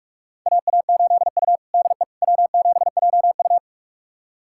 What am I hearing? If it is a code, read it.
Morse code, heard as a keyed tone, about 46 wpm, 700 Hz. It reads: AA8U DE W7JU
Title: USA Callsign DE Callsign 45wpm